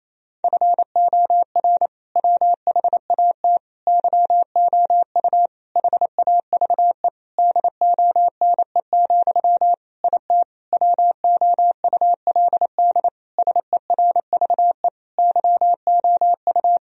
Morse code, heard as a keyed tone, about 27 wpm, 700 Hz. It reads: FOR WHAT YOU HAVE DONE, IT WOULD SERVE YOU
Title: Sentences from Top 1000 Words - Review Part 2 - 28wpm